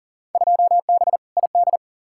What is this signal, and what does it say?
Morse code, heard as a keyed tone, about 40 wpm, 700 Hz. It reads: 2B ID
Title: Field Day Exchanges Encore - 40wpm